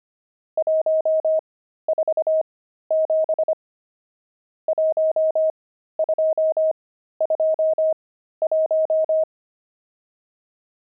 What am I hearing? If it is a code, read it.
Morse code, heard as a keyed tone, about 25 wpm, spaced as for 13 wpm, 630 Hz. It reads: 147 1221
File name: pangram-fs-13-cs-25-nums14